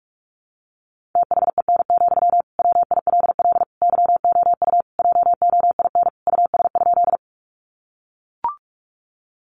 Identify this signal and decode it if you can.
Morse code, heard as a keyed tone, about 44 wpm, 700 Hz. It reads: THEN, WILL YOU JOIN US?
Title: Sentences from Top 1400 Words 45wpm